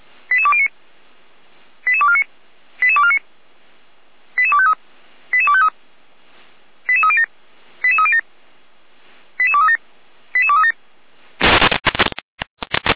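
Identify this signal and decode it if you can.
Five-tone selective call (ZVEI), heard as quick runs of five tones, tones of about 70 ms. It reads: 88209 88249 88249 88253 88253 88307 88307 88237 88237
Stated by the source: ZVEI five-tone call 88392 received by SDR, feuerwehr-orlamuende (fire brigade Orlamünde).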